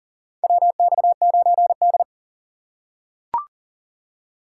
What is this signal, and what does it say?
Morse code, heard as a keyed tone, about 40 wpm, 700 Hz. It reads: WX9D
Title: USA Call Signs - 4 Characters - 3x Character Spacing - 40wpm